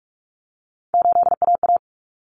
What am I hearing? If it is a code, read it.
Morse code, heard as a keyed tone, about 44 wpm, 700 Hz. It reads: 8AA
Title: Common Cut Numbers - 3-Digit 45wpm